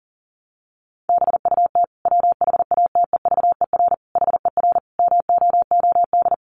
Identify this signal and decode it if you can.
Morse code, heard as a keyed tone, about 40 wpm, 700 Hz. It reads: BUT WHATEVER HER MOOD
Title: Sentences from Top 1100 Words 40wpm